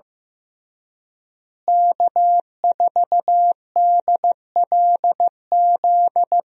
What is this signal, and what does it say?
Morse code, heard as a keyed tone, about 15 wpm, 700 Hz. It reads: K4DLZ
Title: USA Call Signs - 5 Characters - Repeated 3x - 15wpm